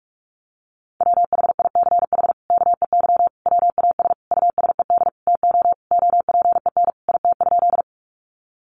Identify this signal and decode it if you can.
Morse code, heard as a keyed tone, about 44 wpm, 700 Hz. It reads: WHICH KEY WAS USED TO OPEN IT?
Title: Sentences from Top 600 Words 45wpm